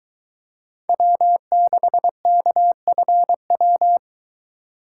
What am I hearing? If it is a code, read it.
Morse code, heard as a keyed tone, about 22 wpm, 700 Hz. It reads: W6KFW